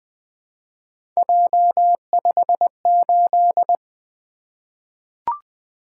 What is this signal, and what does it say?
Morse code, heard as a keyed tone, about 20 wpm, 700 Hz. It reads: J58
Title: Three Letter-Number Combinations - 20wpm